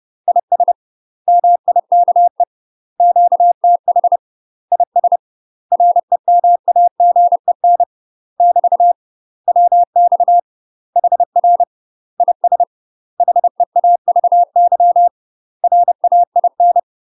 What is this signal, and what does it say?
Morse code, heard as a keyed tone, about 30 wpm, 700 Hz. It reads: IS MIKE QTH IS REMAGEN <BT> WX HR IS HEAVY RAIN
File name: UK MASTER QSOs TRAINING / 30WPM / UK MASTER QSOs 0003